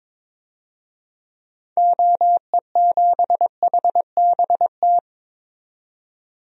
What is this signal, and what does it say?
Morse code, heard as a keyed tone, about 22 wpm, 700 Hz. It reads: OE7HBT
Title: DXCC-100 Call Signs - 6 Characters - 22wpm